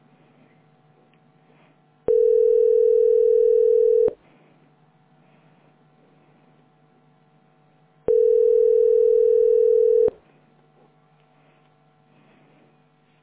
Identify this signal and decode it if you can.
Ringback tone (North America), 440 + 480 Hz.